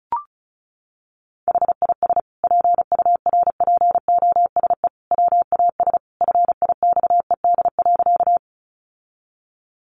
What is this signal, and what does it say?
Morse code, heard as a keyed tone, about 34 wpm, 700 Hz. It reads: HIS PURPOSE WAS FIXED.
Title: Sentences from Top 1100 Words 35wpm